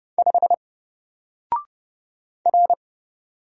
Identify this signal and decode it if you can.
Morse code, heard as a keyed tone, about 31 wpm, 700 Hz. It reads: 5 R
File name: Single-Letter-Number-30wpm